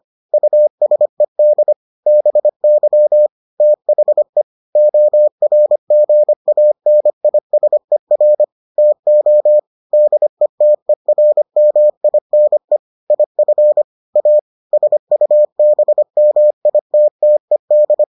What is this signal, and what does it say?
Morse code, heard as a keyed tone, about 25 wpm, 600 Hz. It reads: USED BY THE ORGANISER TO DETERMINE IF A SUBMITTED